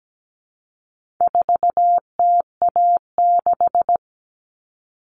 Morse code, heard as a keyed tone, about 17 wpm, 700 Hz. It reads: 4TA6
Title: All Cut Numbers - 4-Digit 17wpm